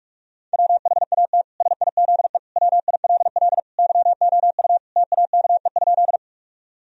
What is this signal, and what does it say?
Morse code, heard as a keyed tone, about 46 wpm, 700 Hz. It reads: WHAT SIZE WILL YOU TAKE?
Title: Sentences from Top 1000 Words - Review Part 2 - 45wpm